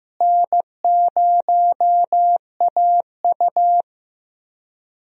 Morse code, heard as a keyed tone, about 15 wpm, 700 Hz. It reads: N0AU
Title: USA Call Signs - 4 Characters - Repeated 3x - 15wpm